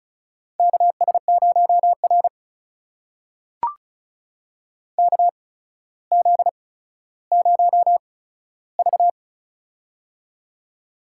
Morse code, heard as a keyed tone, about 34 wpm, spaced as for 15 wpm, 700 Hz. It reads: KS0R KZ0V
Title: USA Call Signs - 4 Characters - 8x Character Spacing - 35wpm